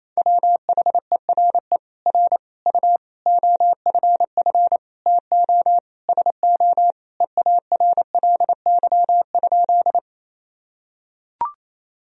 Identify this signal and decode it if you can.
Morse code, heard as a keyed tone, about 28 wpm, 700 Hz. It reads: WHERE R U OFF TO SO EARLY?
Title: Sentences from Top 400 Words and Abbreviations 28wpm